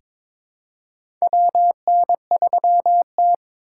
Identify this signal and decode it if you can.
Morse code, heard as a keyed tone, about 22 wpm, 700 Hz. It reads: WN3T
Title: USA Call Signs - 3x Character Spacing - 22wpm